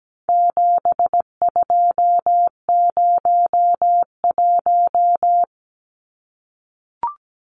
Morse code, heard as a keyed tone, about 17 wpm, 700 Hz. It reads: 7201
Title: Numbers - 4-Digit 17wpm